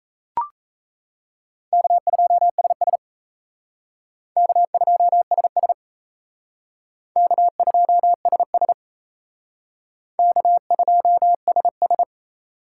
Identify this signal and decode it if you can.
Morse code, heard as a keyed tone, about 37 wpm, 700 Hz. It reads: K2SS K2SS K2SS K2SS